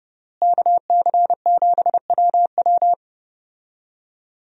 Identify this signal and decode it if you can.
Morse code, heard as a keyed tone, about 32 wpm, 700 Hz. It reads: KC7WW
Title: USA Call Signs - 5 Characters - Repeated 3x - 30wpm